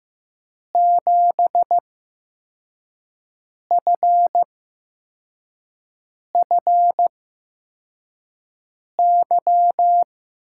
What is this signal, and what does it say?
Morse code, heard as a keyed tone, about 15 wpm, spaced as for 4 wpm, 700 Hz. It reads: 7FFY
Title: USA Call Signs - 5 Characters - 8x Character Spacing - 15wpm